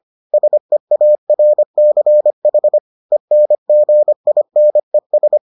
Morse code, heard as a keyed tone, about 25 wpm, 600 Hz. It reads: SEARCH ENGINES